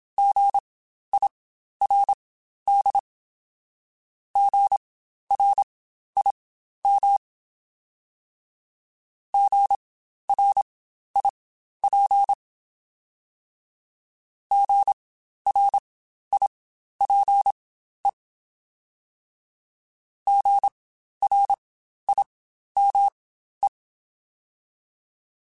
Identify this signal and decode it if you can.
Morse code, heard as a keyed tone, about 27 wpm, spaced as for 12 wpm, 800 Hz. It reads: GIRD GRIM GRIP GRIPE GRIME
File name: Spaced Alphas / Spaced Alphas mp3s / Gs sp @25